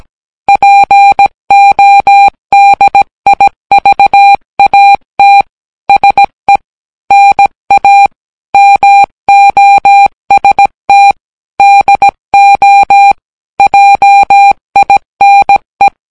Morse code, heard as a keyed tone, about 17 wpm, 800 Hz. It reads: PODIVAT SE NA MOST DO JINE